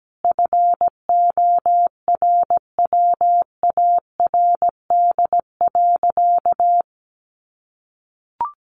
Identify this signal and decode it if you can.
Morse code, heard as a keyed tone, about 17 wpm, 700 Hz. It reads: FORWARD.
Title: Sentences from Top 1300 Words 17wpm